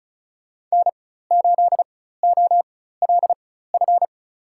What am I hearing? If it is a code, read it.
Morse code, heard as a keyed tone, about 34 wpm, spaced as for 16 wpm, 700 Hz. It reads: N8OLF
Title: USA Call Signs - 5 Characters - 4x Character Spacing - 35wpm